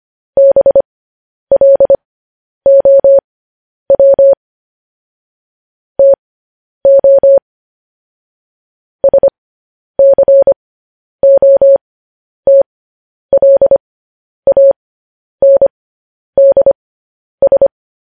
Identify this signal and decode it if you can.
Morse code, heard as a keyed tone, about 25 wpm, spaced as for 10 wpm, 550 Hz. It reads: BLOW TO SCOTLANDS
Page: news headlines in morse code at 10 wpm